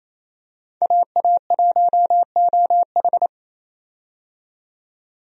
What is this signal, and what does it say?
Morse code, heard as a keyed tone, about 28 wpm, 700 Hz. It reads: AA1OH
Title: USA Call Signs - 5 Characters - 28wpm